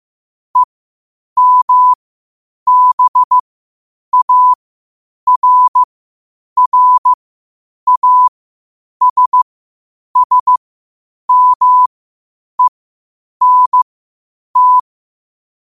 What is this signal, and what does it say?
Morse code, heard as a keyed tone, about 15 wpm, spaced as for 8 wpm, 1000 Hz. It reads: EMBARRASSMENT
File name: qotdfiles / 8wpm